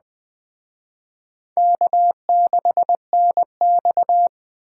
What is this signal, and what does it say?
Morse code, heard as a keyed tone, about 21 wpm, 700 Hz. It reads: K6NX